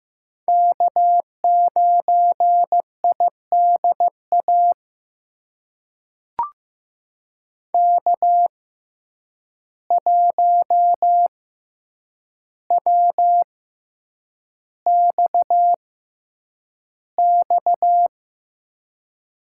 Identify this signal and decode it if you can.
Morse code, heard as a keyed tone, about 15 wpm, spaced as for 8 wpm, 700 Hz. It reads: K9IDA K1WXX
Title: USA Call Signs - 5 Characters - 6x Character Spacing - 15wpm